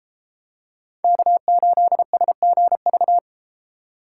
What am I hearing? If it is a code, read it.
Morse code, heard as a keyed tone, about 32 wpm, 700 Hz. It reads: K8SGV